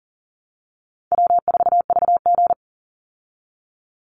Morse code, heard as a keyed tone, about 40 wpm, 700 Hz. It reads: W4VG
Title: Call Signs Repeated 3x - 40wpm